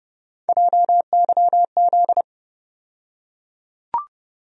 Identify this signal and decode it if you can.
Morse code, heard as a keyed tone, about 32 wpm, 700 Hz. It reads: JYZ
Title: USA Call Sign Postfixes - 3 Letters 30wpm